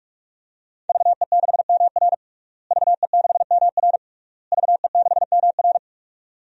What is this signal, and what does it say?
Morse code, heard as a keyed tone, about 46 wpm, 700 Hz. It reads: VE6MR VE6MR VE6MR
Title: DXCC-100 Call Signs - 5 Characters - Repeated 3x - 45wpm